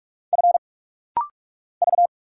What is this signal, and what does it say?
Morse code, heard as a keyed tone, about 44 wpm, 700 Hz. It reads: F V
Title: Single Letter-Number Alt - Rapid-Fire 45wpm